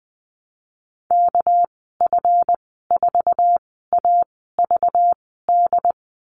Morse code, heard as a keyed tone, about 20 wpm, spaced as for 14 wpm, 700 Hz. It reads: KF4AVD
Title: 219 Lesson 35g - Number 0 - Call Signs 20wpm